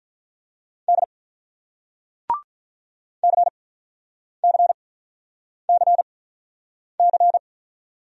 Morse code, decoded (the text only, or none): D C C C C